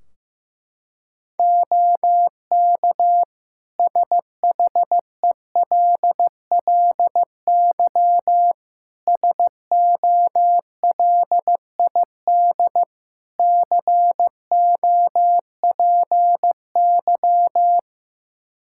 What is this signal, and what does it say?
Morse code, heard as a keyed tone, about 16 wpm, 700 Hz. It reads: OK SHELLY SOLID COPY
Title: Practice QSO - Exchange 3 - More Info - 15wpm